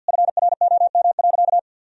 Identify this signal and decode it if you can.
Morse code, heard as a keyed tone, about 50 wpm, 700 Hz. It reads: FROM.
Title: Sentences from Top 1000 Words - Review Part 4 - 50wpm